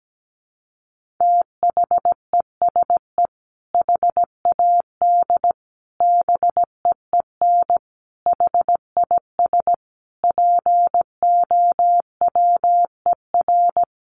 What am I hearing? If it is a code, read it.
Morse code, heard as a keyed tone, about 17 wpm, 700 Hz. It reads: THESE HAD BEEN HIS POWER
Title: Sentences from Top 1500 Words 17wpm